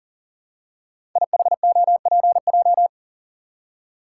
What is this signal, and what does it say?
Morse code, heard as a keyed tone, about 39 wpm, 700 Hz. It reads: IHOPJ
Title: Code Groups - 5 Letters - 40wpm